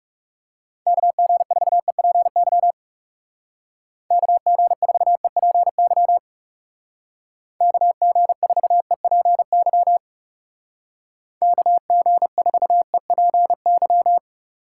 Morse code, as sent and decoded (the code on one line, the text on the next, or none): -.- --. ....- . .--. -.-- / -.- --. ....- . .--. -.-- / -.- --. ....- . .--. -.-- / -.- --. ....- . .--. -.--
KG4EPY KG4EPY KG4EPY KG4EPY